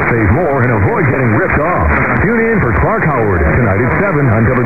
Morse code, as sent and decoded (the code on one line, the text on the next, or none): none